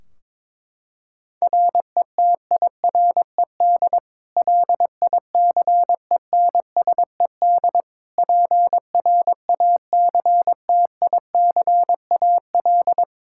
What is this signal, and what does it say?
Morse code, heard as a keyed tone, about 22 wpm, 700 Hz. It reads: RETIRED LICENSED PRACTICAL
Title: K1USN SST QSO Practice 22wpm